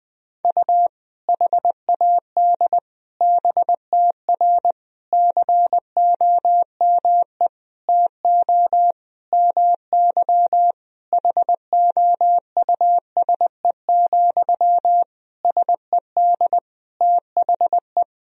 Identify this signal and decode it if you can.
Morse code, heard as a keyed tone, about 20 wpm, 700 Hz. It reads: U HAD BTR COME TO MY HOUSE, SED THE
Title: Sentences from Top 400 Words and Abbreviations 20wpm